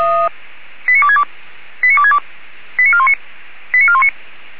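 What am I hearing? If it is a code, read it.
Five-tone selective call (ZVEI), heard as quick runs of five tones, tones of about 70 ms. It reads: 89262 89262 89419 89419
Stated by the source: ZVEI five-tone selective call received on SDR.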